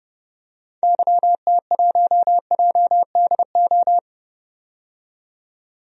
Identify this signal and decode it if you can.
Morse code, heard as a keyed tone, about 30 wpm, 700 Hz. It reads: YT1JDO
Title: DXCC-100 Call Signs - 6 Characters - 30wpm